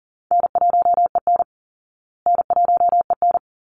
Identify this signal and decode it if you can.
Morse code, heard as a keyed tone, about 40 wpm, 700 Hz. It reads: N1EN N1EN
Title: Call Signs Repeated 3x - 40wpm